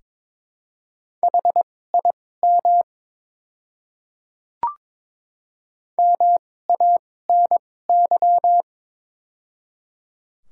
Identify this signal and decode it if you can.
Morse code, heard as a keyed tone, about 22 wpm, spaced as for 16 wpm, 700 Hz. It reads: HIM MANY
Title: 2x Character Spacing - Top 100 Words - 22wpm